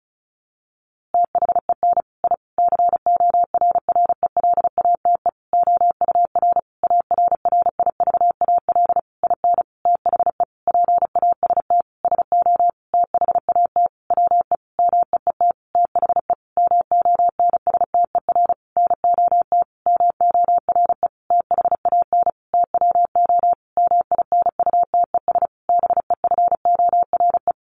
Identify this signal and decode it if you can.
Morse code, heard as a keyed tone, about 34 wpm, 700 Hz. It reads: THEN I CORRELATE OUR ARRIVAL IN THE PAST SO THAT WE MEET THE MONSTER NOT MORE THAN TWO MINUTES BEFORE
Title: A Sound Of Thunder 35wpm